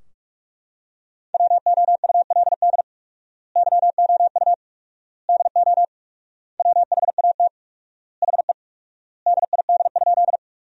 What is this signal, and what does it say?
Morse code, heard as a keyed tone, about 41 wpm, 700 Hz. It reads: WOULD YOU DO WHAT HE DID?
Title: 4x Word-Spacing - Sentences from Top 100 Words - 45wpm